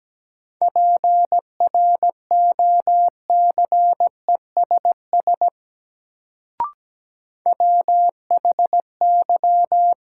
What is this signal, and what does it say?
Morse code, heard as a keyed tone, about 17 wpm, 700 Hz. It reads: PROCESS WHY
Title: Sets of 3 Words - New Word Drill - 17wpm